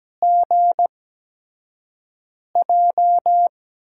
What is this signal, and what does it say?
Morse code, heard as a keyed tone, about 17 wpm, spaced as for 5 wpm, 700 Hz. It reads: GJ
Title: USA Call Signs - 6 Characters - 8x Character Spacing - 17wpm